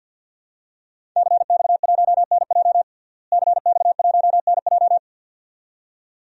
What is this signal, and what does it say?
Morse code, heard as a keyed tone, about 48 wpm, 700 Hz. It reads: CX1NJ CX1NJ
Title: DXCC-100 Call Signs - 5 Characters - Repeated 2x - 50wpm